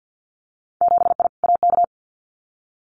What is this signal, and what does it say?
Morse code, heard as a keyed tone, about 50 wpm, 700 Hz. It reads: 7I AK
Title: Winter Field Day Exchanges Encore 50wpm